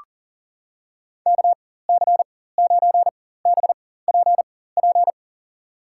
Morse code, heard as a keyed tone, about 40 wpm, spaced as for 19 wpm, 700 Hz. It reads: KC9BPP
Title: USA Call Signs - 6 Characters - 4x Character Spacing - 40wpm